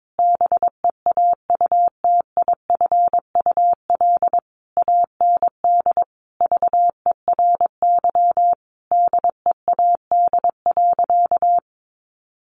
Morse code, heard as a keyed tone, about 22 wpm, 700 Hz. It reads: BEAUTIFUL AND VERY DEAD.
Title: A Sound Of Thunder 22wpm